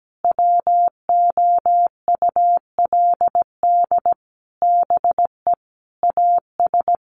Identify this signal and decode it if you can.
Morse code, heard as a keyed tone, about 17 wpm, 700 Hz. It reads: WOULD BE AS
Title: Sentences from Top 1200 Words 17wpm